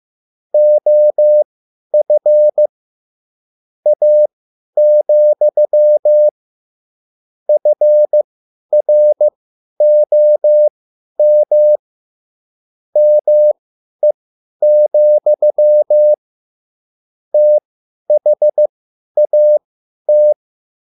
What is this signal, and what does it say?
Morse code, heard as a keyed tone, about 15 wpm, spaced as for 10 wpm, 600 Hz. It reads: OF A, FROM ME, THAT